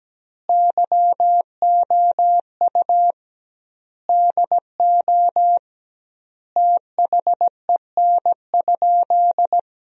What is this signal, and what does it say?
Morse code, heard as a keyed tone, about 17 wpm, 700 Hz. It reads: YOU DO THEN?
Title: 2x Word-Spacing - Sentences from Top 100 Words - 17wpm